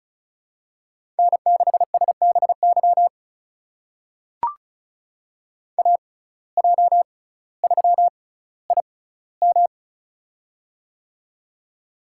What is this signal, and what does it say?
Morse code, heard as a keyed tone, about 36 wpm, spaced as for 18 wpm, 700 Hz. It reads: N6SBY AJ3IM
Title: USA Call Signs - 5 Characters - 6x Character Spacing - 35wpm